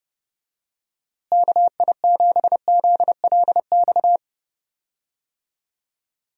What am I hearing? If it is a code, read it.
Morse code, heard as a keyed tone, about 30 wpm, 700 Hz. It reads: KI7ZLX
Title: USA Call Signs - 6 Characters - Repeated 2x - 30wpm